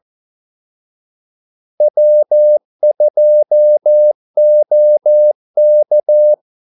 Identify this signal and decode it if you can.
Morse code, heard as a keyed tone, about 14 wpm, 600 Hz. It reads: W2OK
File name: Callsigns-14wpm